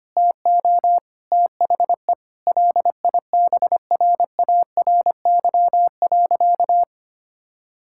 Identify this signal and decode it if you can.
Morse code, heard as a keyed tone, about 25 wpm, 700 Hz. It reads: TO THE LIBRARY.
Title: Sentences from Top 1000 Words - Review Part 6 - 25wpm